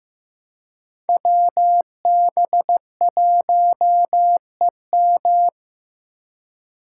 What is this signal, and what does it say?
Morse code, heard as a keyed tone, about 15 wpm, 700 Hz. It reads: WB1EM
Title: Code Groups - 5 Letter:Numbers - 15wpm